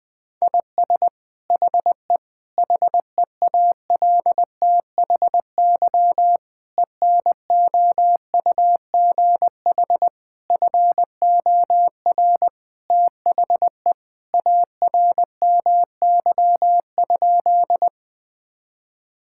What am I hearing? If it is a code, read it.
Morse code, heard as a keyed tone, about 20 wpm, 700 Hz. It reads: IS HE HEALTHY ENOUGH FOR THE ARMY?
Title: Sentences from Top 1000 Words - Review Part 5 - 20wpm